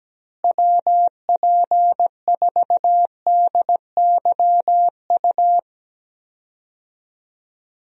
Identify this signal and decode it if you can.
Morse code, heard as a keyed tone, about 17 wpm, 700 Hz. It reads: WP4DYU
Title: USA Call Signs - 6 Characters - Repeated 2x - 17wpm